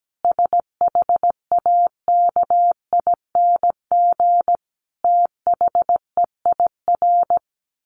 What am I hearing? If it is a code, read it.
Morse code, heard as a keyed tone, about 17 wpm, 700 Hz. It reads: SHAKING THEIR
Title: A Sound Of Thunder 17wpm